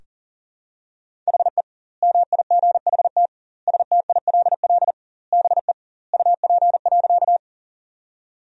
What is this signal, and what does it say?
Morse code, heard as a keyed tone, about 37 wpm, 700 Hz. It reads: HE MIGHT STILL BE UP.
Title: 2x Word-Spacing - Sentences from Top 300 Words - 40wpm